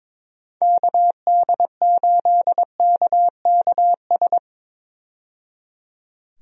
Morse code, heard as a keyed tone, about 22 wpm, 700 Hz. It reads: KD8KKS
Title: USA Call Signs - 6 Characters - 22wpm